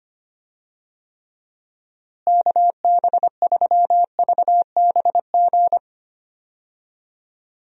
Morse code, heard as a keyed tone, about 25 wpm, 700 Hz. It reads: KB3VBG
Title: USA Call Signs - 6 Characters - Repeated 3x - 25wpm